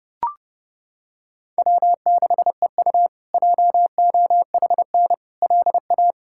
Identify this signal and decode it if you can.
Morse code, heard as a keyed tone, about 30 wpm, 700 Hz. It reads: W6EU JOHN LA
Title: K1USN SST QSO Practice - 30wpm